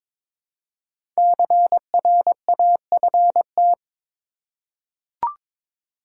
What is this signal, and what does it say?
Morse code, heard as a keyed tone, about 22 wpm, 700 Hz. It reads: CRAFT
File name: 5 Letter Words - First 3 Letters Given - 22wpm